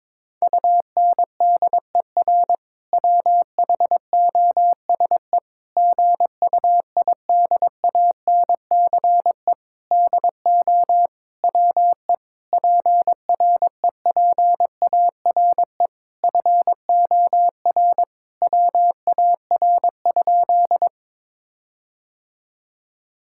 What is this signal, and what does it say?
Morse code, heard as a keyed tone, about 22 wpm, 700 Hz. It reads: UNDER WHOSE GUIDANCE DO WE PREPARE FOR WAR?
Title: Sentences from Top 1600 Words 22wpm